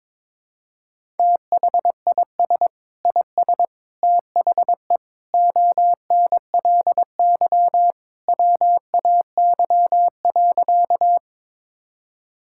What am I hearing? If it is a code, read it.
Morse code, heard as a keyed tone, about 22 wpm, 700 Hz. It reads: THIS IS THE ONLY WAY.
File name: practice / 22wpm / Sentences with 2-Word Phrases - 22wpm